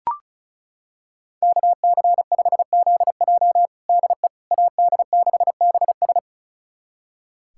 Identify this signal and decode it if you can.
Morse code, heard as a keyed tone, about 34 wpm, 700 Hz. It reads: KC5ZJ DE AD6BS
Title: USA Callsign DE Callsign - 5 Characters - 35wpm